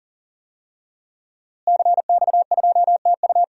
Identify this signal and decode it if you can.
Morse code, heard as a keyed tone, about 40 wpm, 700 Hz. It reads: CX2TU
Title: DXCC-100 Call Signs - 5 Characters - Repeated 2x - 40wpm